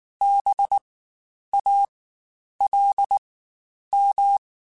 Morse code, heard as a keyed tone, about 19 wpm, spaced as for 9 wpm, 800 Hz. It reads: BALM
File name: Spaced Alphas / Spaced Alphas mp3s / Bs sp @18